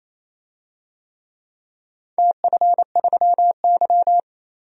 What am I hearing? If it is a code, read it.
Morse code, heard as a keyed tone, about 28 wpm, 700 Hz. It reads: TF3Y
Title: DXCC-100 Call Signs - 4 Characters - 28wpm